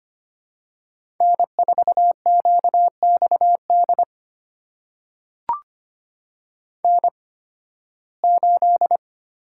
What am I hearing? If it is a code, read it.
Morse code, heard as a keyed tone, about 25 wpm, spaced as for 16 wpm, 700 Hz. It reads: N4QXD N8
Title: USA Call Signs - 5 Characters - 8x Character Spacing - 25wpm